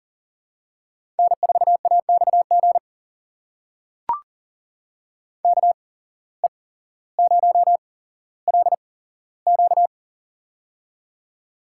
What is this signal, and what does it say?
Morse code, heard as a keyed tone, about 40 wpm, spaced as for 17 wpm, 700 Hz. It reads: N4AXG KE0LQ
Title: USA Call Signs - 5 Characters - 8x Character Spacing - 40wpm